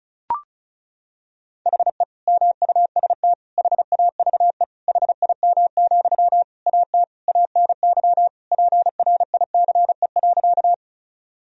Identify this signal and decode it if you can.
Morse code, heard as a keyed tone, about 36 wpm, 700 Hz. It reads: HE MUST HAVE HIM, AT ANY PRICE.
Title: Sentences from Top 1000 Words - Review Part 4 - 35wpm